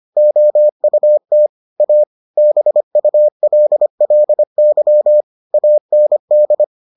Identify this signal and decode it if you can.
Morse code, heard as a keyed tone, about 25 wpm, 600 Hz. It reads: OUT A BULLY AND